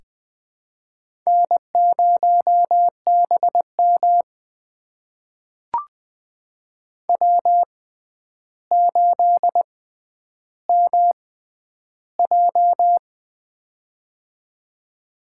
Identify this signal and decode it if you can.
Morse code, heard as a keyed tone, about 20 wpm, spaced as for 10 wpm, 700 Hz. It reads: N0BM W8MJ
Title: USA Call Signs - 6x Character Spacing - 20wpm